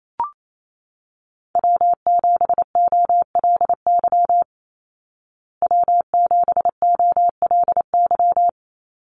Morse code, heard as a keyed tone, about 28 wpm, 700 Hz. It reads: W7OLY W7OLY
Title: Call Signs Repeated 2x - 28wpm